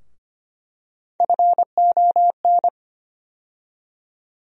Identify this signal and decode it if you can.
Morse code, heard as a keyed tone, about 25 wpm, 700 Hz. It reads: FON
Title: USA Call Sign Postfixes - 3 Letters 25wpm